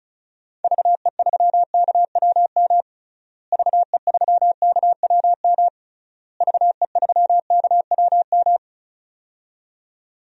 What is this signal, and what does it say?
Morse code, heard as a keyed tone, about 34 wpm, 700 Hz. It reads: VE3KWM VE3KWM VE3KWM
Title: DXCC-100 Call Signs - 6 Characters - Repeated 3x - 35wpm